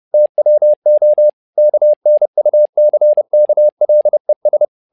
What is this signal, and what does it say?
Morse code, heard as a keyed tone, about 30 wpm, 600 Hz. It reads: TWO KNUCKLES
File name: learning center / tower trouble / tower trouble 30wpm